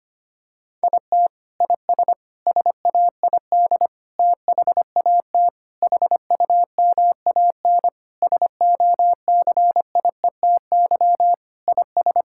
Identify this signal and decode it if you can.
Morse code, heard as a keyed tone, about 25 wpm, 700 Hz. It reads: IT IS SAID THAT HUMAN SOCIETY IS
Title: Sentences from Top 1000 Words - Review Part 6 - 25wpm